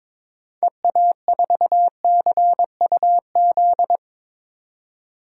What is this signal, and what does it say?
Morse code, heard as a keyed tone, about 21 wpm, 700 Hz. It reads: EA4CUZ
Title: DXCC-100 Call Signs - 6 Characters - 22wpm